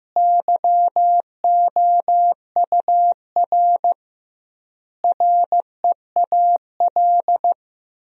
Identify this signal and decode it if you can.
Morse code, heard as a keyed tone, about 15 wpm, 700 Hz. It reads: YOUR REAL
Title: 2x Word-Spacing - Sentences from Top 300 Words - 15wpm